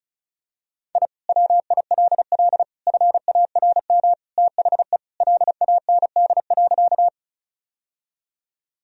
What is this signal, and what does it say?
Morse code, heard as a keyed tone, about 36 wpm, 700 Hz. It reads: I WILL FARM THE LAND.
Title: Sentences from Top 1000 Words - Review Part 2 - 35wpm